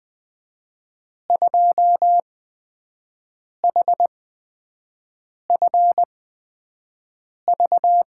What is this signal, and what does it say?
Morse code, heard as a keyed tone, about 20 wpm, spaced as for 5 wpm, 700 Hz. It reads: 2HFV
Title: USA Call Signs - 8x Character Spacing - 20wpm